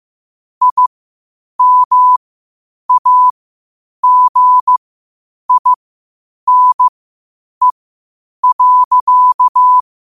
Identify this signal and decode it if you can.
Morse code, heard as a keyed tone, about 15 wpm, spaced as for 8 wpm, 1000 Hz. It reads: IMAGINE.
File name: qotdfiles / 8wpm